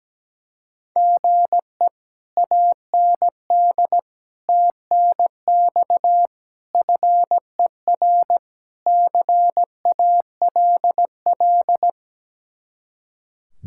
Morse code, heard as a keyed tone, about 18 wpm, 700 Hz. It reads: GE AND TNX FER CALL
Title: Speed-Racing Common QSO Phrases - 17wpm